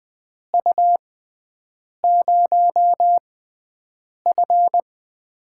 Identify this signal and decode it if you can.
Morse code, heard as a keyed tone, about 20 wpm, spaced as for 7 wpm, 700 Hz. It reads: U0F
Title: USA Call Signs - 4 Characters - 6x Character Spacing - 20wpm